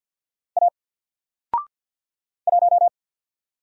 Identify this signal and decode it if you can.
Morse code, heard as a keyed tone, about 49 wpm, 700 Hz. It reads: A 1
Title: Single Character (Letter:Number:Punctuation:Prosign) 50wpm